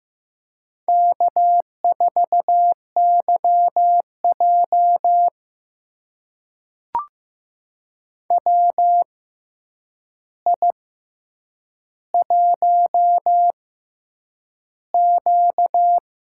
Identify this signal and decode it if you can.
Morse code, heard as a keyed tone, about 15 wpm, spaced as for 8 wpm, 700 Hz. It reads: K4YJ WI1Q
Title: USA Call Signs - 4 Characters - 6x Character Spacing - 15wpm